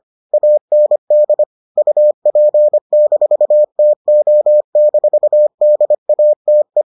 Morse code, heard as a keyed tone, about 25 wpm, 600 Hz. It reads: AND UP-TO-DATE